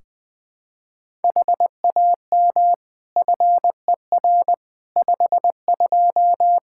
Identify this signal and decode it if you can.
Morse code, heard as a keyed tone, about 21 wpm, 700 Hz. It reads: HAM FER 52
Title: Practice QSO - Exchange 3 - More Info - 20wpm